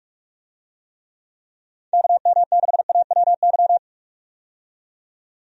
Morse code, heard as a keyed tone, about 46 wpm, 700 Hz. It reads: KM6AWY
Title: USA Call Signs - 6 Characters - 45wpm